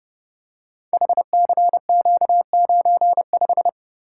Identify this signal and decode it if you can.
Morse code, heard as a keyed tone, about 30 wpm, 700 Hz. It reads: HCQ95
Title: Code Groups - 5 Letter:Numbers - 30wpm